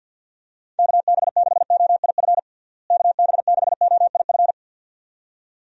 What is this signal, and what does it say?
Morse code, heard as a keyed tone, about 53 wpm, 700 Hz. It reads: KB6OIF KB6OIF
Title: USA Call Signs - 6 Characters - Repeated 2x - 50wpm